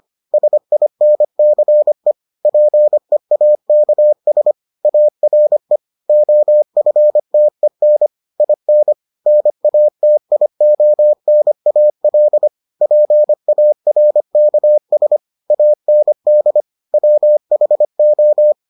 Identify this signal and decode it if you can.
Morse code, heard as a keyed tone, about 25 wpm, 600 Hz. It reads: SINCE PEAKS ARE OFTEN IN NATIONAL PARKS AND WHO